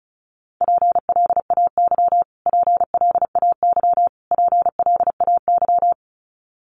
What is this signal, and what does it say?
Morse code, heard as a keyed tone, about 34 wpm, 700 Hz. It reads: PLAY PLAY PLAY
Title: Top 500 Words Repeated 35wpm